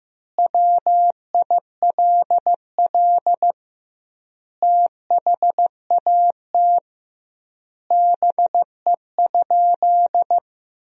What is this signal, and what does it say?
Morse code, heard as a keyed tone, about 15 wpm, 700 Hz. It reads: WILL THAT BE?
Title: 2x Word-Spacing - Sentences from Top 100 Words - 15wpm